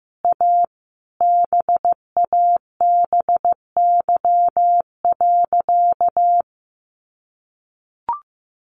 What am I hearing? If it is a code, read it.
Morse code, heard as a keyed tone, about 15 wpm, 700 Hz. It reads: A BABY.
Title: Sentences from Top 1400 Words 15wpm